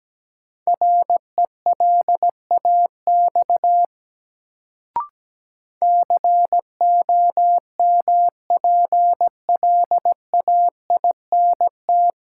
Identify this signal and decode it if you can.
Morse code, heard as a keyed tone, about 17 wpm, 700 Hz. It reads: RELAX COMPLAINT
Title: Sentences from Top 1400 Words - New Word Drill - 17wpm